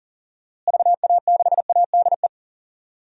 Morse code, heard as a keyed tone, about 40 wpm, 700 Hz. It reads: VA6ADE
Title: DXCC-100 Call Signs - 6 Characters - Repeated 2x - 40wpm